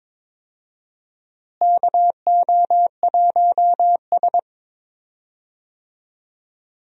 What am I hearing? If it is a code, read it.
Morse code, heard as a keyed tone, about 22 wpm, 700 Hz. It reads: KO1S
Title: USA Call Signs - 4 Characters - Repeated 3x - 22wpm